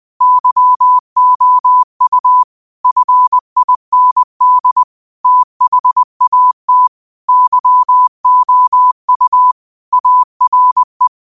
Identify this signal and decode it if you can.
Morse code, heard as a keyed tone, about 20 wpm, 1000 Hz. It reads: YOU FIND THAT YOU ARE
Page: Daily quotes in Morse Code at 20 Words per Minute. Use this to improve your code copying skill.